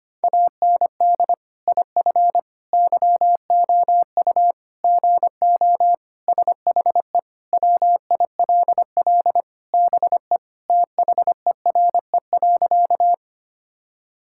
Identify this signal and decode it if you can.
Morse code, heard as a keyed tone, about 25 wpm, 700 Hz. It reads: AND IF YOU GO SHE WILL BE THERE.
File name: practice / 25wpm / Sentences with 3-Word Phrases - 25wpm